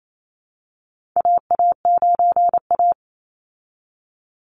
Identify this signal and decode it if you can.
Morse code, heard as a keyed tone, about 28 wpm, 700 Hz. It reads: AA9A
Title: Call Signs Repeated 2x - 28wpm